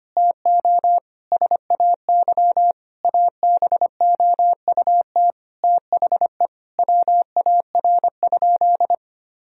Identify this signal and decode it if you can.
Morse code, heard as a keyed tone, about 25 wpm, 700 Hz. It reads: TO SAY ABOUT THE WAR?
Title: Sentences from Top 1000 Words - Review Part 4 - 25wpm